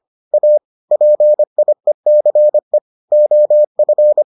Morse code, heard as a keyed tone, about 25 wpm, 600 Hz. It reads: A PIECE OF